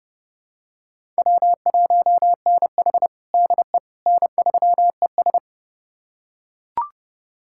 Morse code, heard as a keyed tone, about 30 wpm, 700 Hz. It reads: W1NH DE N3ES
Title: USA Callsign DE Callsign - 4 Characters - 30wpm